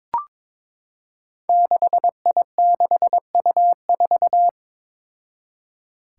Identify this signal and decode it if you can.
Morse code, heard as a keyed tone, about 21 wpm, 700 Hz. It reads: 6I6U4
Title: Five Challenging Letter-Number Combinations 22wpm